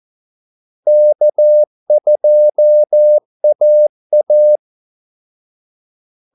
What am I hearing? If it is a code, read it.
Morse code, heard as a keyed tone, about 14 wpm, 600 Hz. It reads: K2AA
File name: Callsigns-14wpm